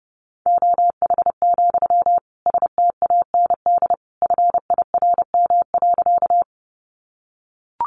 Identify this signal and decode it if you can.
Morse code, heard as a keyed tone, about 30 wpm, 700 Hz. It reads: OH, STAND FIRM.
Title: Sentences from Top 1400 Words 30wpm